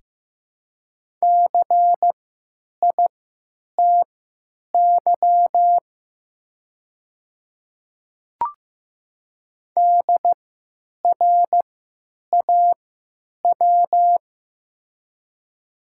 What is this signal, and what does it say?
Morse code, heard as a keyed tone, about 15 wpm, spaced as for 9 wpm, 700 Hz. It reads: CITY DRAW
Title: 3x Character Spacing - Top 500 Words - 15wpm